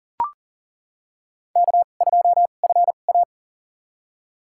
Morse code, heard as a keyed tone, about 40 wpm, spaced as for 29 wpm, 700 Hz. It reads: K2FA
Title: USA Call Signs - 4 Characters - 2x Character Spacing - 40wpm